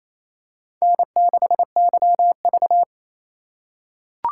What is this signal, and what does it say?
Morse code, heard as a keyed tone, about 28 wpm, 700 Hz. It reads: N6YV
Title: USA Call Signs - 4 Characters - 3x Character Spacing - 28wpm